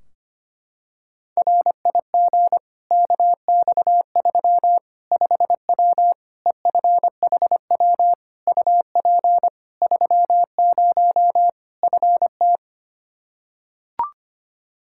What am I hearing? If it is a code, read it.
Morse code, heard as a keyed tone, about 25 wpm, 700 Hz. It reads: RIG KX3 5W EFHW UP 30 FT
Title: Practice QSO - Exchange 2 - Info - 25wpm